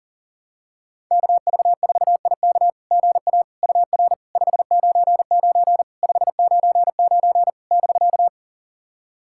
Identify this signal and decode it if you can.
Morse code, heard as a keyed tone, about 40 wpm, 700 Hz. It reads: KV4IK GA UR 599 599 <BK>
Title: SOTA QSO Practice 40wpm